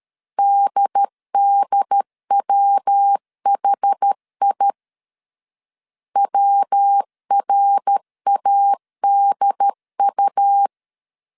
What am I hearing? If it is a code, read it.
Morse code, heard as a keyed tone, about 13 wpm, 800 Hz. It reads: DDWHI WRADU
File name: CW / DET / 1 3 1500C S B 12wpm 2828